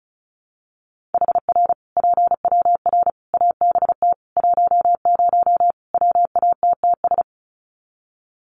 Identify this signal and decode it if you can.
Morse code, heard as a keyed tone, about 34 wpm, 700 Hz. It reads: HR PWR ABT 10 WATTS
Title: Common QSO Phrases 35wpm